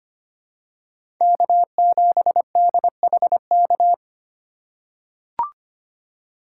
Morse code, heard as a keyed tone, about 25 wpm, 700 Hz. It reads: K7DHK
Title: USA Call Signs - 5 Characters - 2x Character Spacing - 25wpm